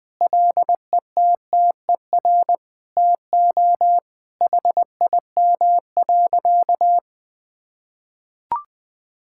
Morse code, heard as a keyed tone, about 20 wpm, 700 Hz. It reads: LETTER TO HIM.
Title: Sentences with 2-Word Phrases - Spoken Beginning - 20wpm